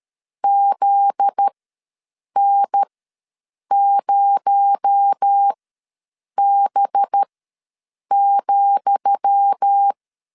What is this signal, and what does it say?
Morse code, heard as a keyed tone, about 13 wpm, spaced as for 7 wpm, 800 Hz. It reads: ZN0B,
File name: LEZIONI / 10 SIGNS / 10D COMA ws 12wpm 3618